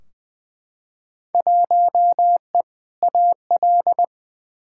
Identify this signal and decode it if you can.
Morse code, heard as a keyed tone, about 21 wpm, 700 Hz. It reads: 1E AL
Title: Field Day Exchanges - 20wpm